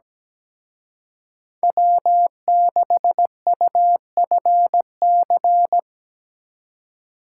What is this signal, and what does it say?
Morse code, heard as a keyed tone, about 19 wpm, 700 Hz. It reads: W6UFC
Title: Speed Racing - USA 1x3 Call Signs 17wpm